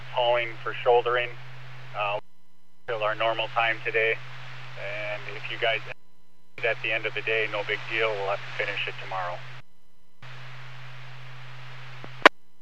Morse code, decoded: none

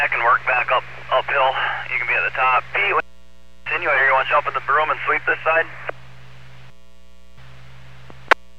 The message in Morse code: none

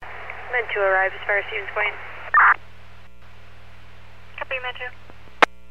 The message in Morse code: none

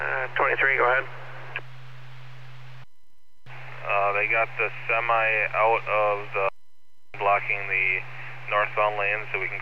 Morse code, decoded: none

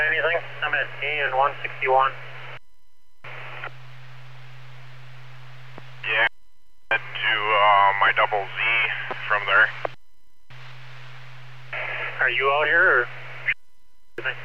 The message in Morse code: none

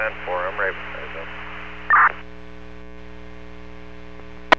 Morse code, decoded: none

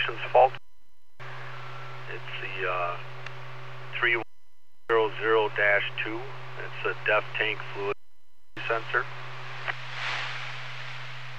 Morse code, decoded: none

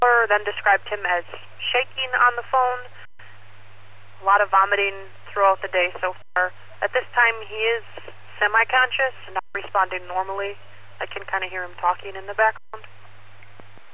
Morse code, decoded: none

